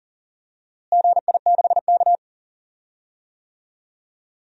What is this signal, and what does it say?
Morse code, heard as a keyed tone, about 40 wpm, 700 Hz. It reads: GI6K
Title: DXCC-100 Call Signs - 4 Characters - Repeated 2x - 40wpm